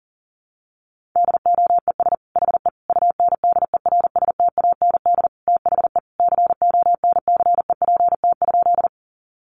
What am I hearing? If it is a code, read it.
Morse code, heard as a keyed tone, about 40 wpm, 700 Hz. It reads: DOES HE UNDERSTAND THE CONCEPT?
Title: Sentences from Top 1100 Words 40wpm